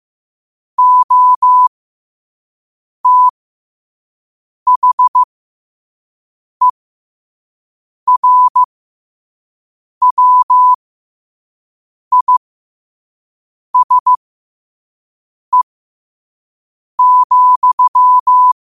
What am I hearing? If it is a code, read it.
Morse code, heard as a keyed tone, about 15 wpm, spaced as for 5 wpm, 1000 Hz. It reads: OTHERWISE,
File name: qotdfiles / 5wpm